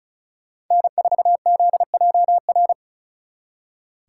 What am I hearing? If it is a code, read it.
Morse code, heard as a keyed tone, about 46 wpm, 700 Hz. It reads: N4ZJR